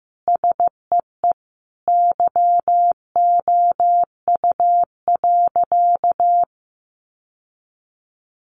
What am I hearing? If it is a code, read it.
Morse code, heard as a keyed tone, about 15 wpm, 700 Hz. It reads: SEE YOU.
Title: Sentences from Top 800 Words 15wpm